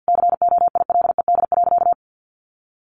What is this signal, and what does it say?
Morse code, heard as a keyed tone, about 48 wpm, 700 Hz. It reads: COILED.
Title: A Sound Of Thunder 50wpm